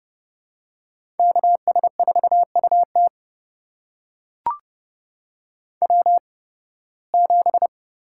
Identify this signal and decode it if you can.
Morse code, heard as a keyed tone, about 31 wpm, spaced as for 20 wpm, 700 Hz. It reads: KS4UT W7
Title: USA Call Signs - 8x Character Spacing - 30wpm